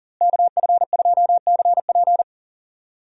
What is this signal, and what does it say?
Morse code, heard as a keyed tone, about 37 wpm, 700 Hz. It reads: KF2CP
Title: USA Call Signs - 5 Characters - Repeated 3x - 40wpm